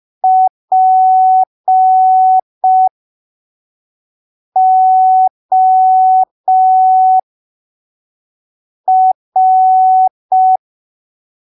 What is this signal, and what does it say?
Morse code, heard as a keyed tone, about 5 wpm, spaced as for 3 wpm, 750 Hz. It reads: POR
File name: arquivos / texto / texto 17 05ppm